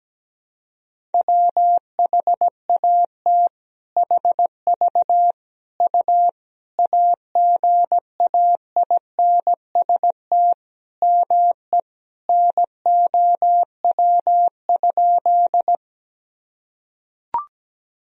Morse code, heard as a keyed tone, about 17 wpm, 700 Hz. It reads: WHAT HV U AGAINST ME NOW?
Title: Sentences from Top 400 Words and Abbreviations 17wpm